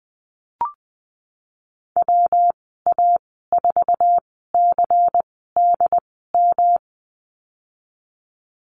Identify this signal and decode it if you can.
Morse code, heard as a keyed tone, about 20 wpm, spaced as for 15 wpm, 700 Hz. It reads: WA4CDM
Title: 186 Lesson 30g - Letter b - Call Signs 20wpm